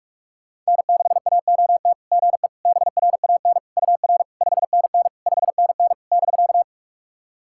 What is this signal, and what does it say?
Morse code, heard as a keyed tone, about 46 wpm, 690 Hz. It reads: N6AOT GE BRAN UR 5NN 5NN <BK>
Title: SOTA QSO Practice 45wpm